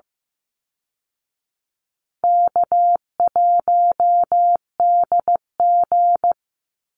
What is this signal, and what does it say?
Morse code, heard as a keyed tone, about 14 wpm, 700 Hz. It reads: K1DG
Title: Easy Call Signs Repeated 3x - 15wpm